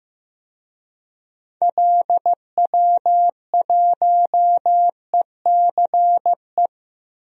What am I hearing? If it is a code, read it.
Morse code, heard as a keyed tone, about 15 wpm, 700 Hz. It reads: LW1ECE
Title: DXCC-100 Call Signs - 6 Characters - Repeated 3x - 15wpm